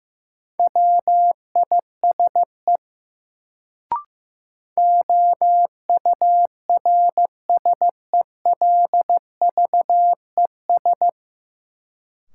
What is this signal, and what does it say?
Morse code, heard as a keyed tone, about 15 wpm, 700 Hz. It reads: WISE OURSELVES
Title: Sentences from Top 1600 Words - New Word Drill - 15wpm